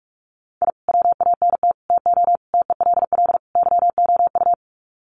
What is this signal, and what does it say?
Morse code, heard as a keyed tone, about 44 wpm, 700 Hz. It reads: I WANT TO TELL YOU
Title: Sets of 5 Words Encore 45wpm